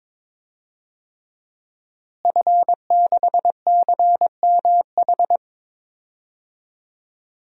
Morse code, heard as a keyed tone, about 21 wpm, 700 Hz. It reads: F6CMH
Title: DXCC-100 Call Signs - 5 Characters - Repeated 3x - 22wpm